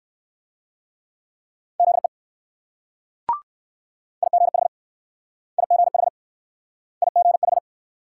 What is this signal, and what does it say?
Morse code, heard as a keyed tone, about 56 wpm, 690 Hz. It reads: ZE IGH IGH IGH
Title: Speed-Racing Multi-letter Phonemes - 45wpm